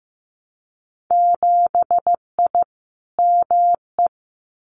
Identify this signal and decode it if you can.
Morse code, heard as a keyed tone, about 15 wpm, 700 Hz. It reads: 7I ME
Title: Winter Field Day Exchanges Encore 15wpm